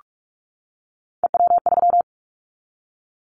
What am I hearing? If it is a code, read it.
Morse code, heard as a keyed tone, about 46 wpm, 700 Hz. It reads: EW3
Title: Callsign Prefixes for Top 50 DXCC - 45wpm